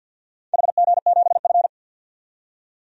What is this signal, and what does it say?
Morse code, heard as a keyed tone, about 49 wpm, 700 Hz. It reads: HG7F
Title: DXCC-100 Call Signs - 4 Characters - 50wpm